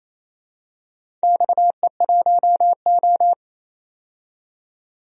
Morse code, heard as a keyed tone, about 28 wpm, 700 Hz. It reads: XE1O
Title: DXCC-100 Call Signs - 4 Characters - 28wpm